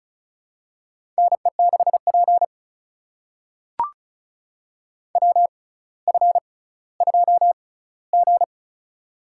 Morse code, heard as a keyed tone, about 36 wpm, spaced as for 18 wpm, 700 Hz. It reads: NE6P WF2G
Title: USA Call Signs - 4 Characters - 6x Character Spacing - 35wpm